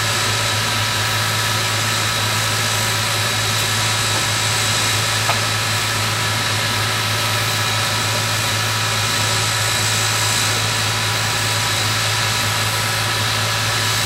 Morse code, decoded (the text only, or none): none